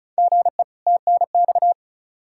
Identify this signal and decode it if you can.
Morse code, heard as a keyed tone, about 36 wpm, 700 Hz. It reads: GE TNX
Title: Practice QSO - Exchange 1 - Beginning - 35wpm